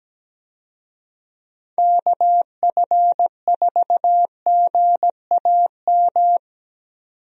Morse code, heard as a keyed tone, about 17 wpm, 700 Hz. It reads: KF4GAM
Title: USA Call Signs - 6 Characters - 17wpm